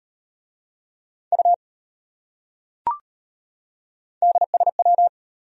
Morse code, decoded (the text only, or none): U DSW